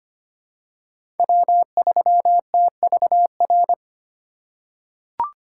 Morse code, heard as a keyed tone, about 25 wpm, 700 Hz. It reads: W3TVR
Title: USA Call Signs - 5 Characters - 3x Character Spacing - 25wpm